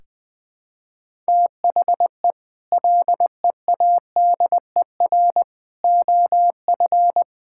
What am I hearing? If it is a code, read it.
Morse code, heard as a keyed tone, about 20 wpm, 700 Hz. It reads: THE LEADER OF
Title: Sets of 6 Words - 20wpm